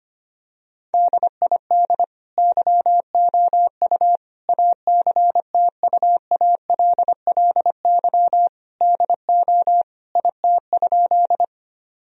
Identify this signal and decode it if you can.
Morse code, heard as a keyed tone, about 25 wpm, 700 Hz. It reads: DID YOU ACTUALLY DO IT?
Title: Sentences from Top 1000 Words - Review Part 4 - 25wpm